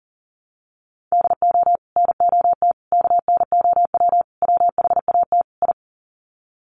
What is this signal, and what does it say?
Morse code, heard as a keyed tone, about 40 wpm, 700 Hz. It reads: DO NOT KNOW WHAT I
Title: Sets of 5 Words 40wpm